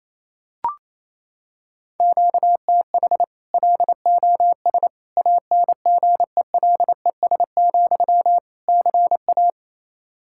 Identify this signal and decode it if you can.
Morse code, heard as a keyed tone, about 28 wpm, 700 Hz. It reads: QTH LOS ANGELES, CA
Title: Practice QSO - Exchange 1 - Beginning - 28wpm